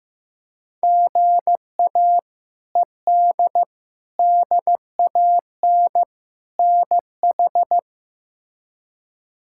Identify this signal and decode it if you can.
Morse code, heard as a keyed tone, about 14 wpm, 700 Hz. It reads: GA ED DAN NH
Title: K1USN SST QSO Practice - 15wpm